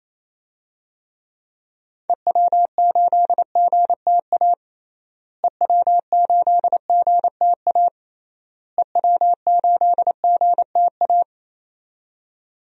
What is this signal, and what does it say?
Morse code, heard as a keyed tone, about 28 wpm, 700 Hz. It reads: EW8GTA EW8GTA EW8GTA
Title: DXCC-100 Call Signs - 6 Characters - Repeated 3x - 28wpm